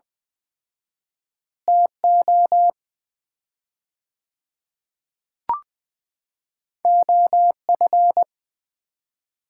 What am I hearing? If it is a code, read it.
Morse code, heard as a keyed tone, about 20 wpm, 700 Hz. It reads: TO OF
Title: 4x Word-Spacing - Sets of 2 Words - 20wpm